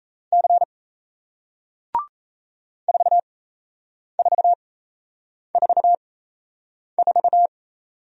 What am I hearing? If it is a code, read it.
Morse code, heard as a keyed tone, about 37 wpm, spaced as for 8 wpm, 700 Hz. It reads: C 4444